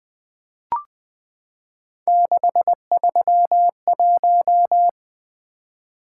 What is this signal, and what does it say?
Morse code, heard as a keyed tone, about 20 wpm, 700 Hz. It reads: 631
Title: Three Letter-Number Combinations - 20wpm